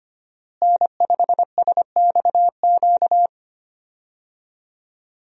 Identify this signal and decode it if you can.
Morse code, heard as a keyed tone, about 25 wpm, 700 Hz. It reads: N5SXQ
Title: USA Call Signs - 5 Characters - Repeated 2x - 25wpm